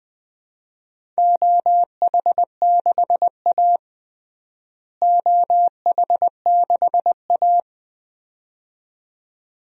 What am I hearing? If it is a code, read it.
Morse code, heard as a keyed tone, about 21 wpm, 700 Hz. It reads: OH6A OH6A
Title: DXCC-100 Call Signs - 4 Characters - Repeated 2x - 20wpm